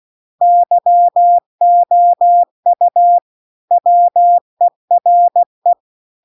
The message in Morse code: YOU WERE

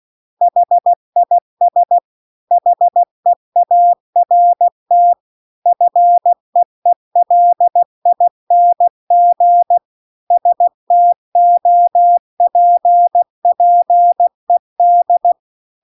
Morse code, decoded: HIS HEART FEELING STOPPED

